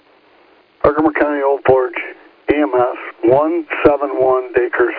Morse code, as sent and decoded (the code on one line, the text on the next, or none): none